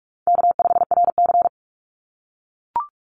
Morse code, heard as a keyed tone, about 44 wpm, 700 Hz. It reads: K5RC
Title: Call Signs - 45wpm